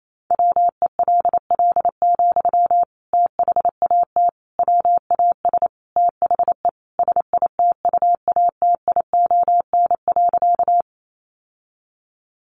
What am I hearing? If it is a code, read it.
Morse code, heard as a keyed tone, about 28 wpm, 700 Hz. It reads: WELL, THAT WAS THE SITUATION.
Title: Sentences from Top 1200 Words 28wpm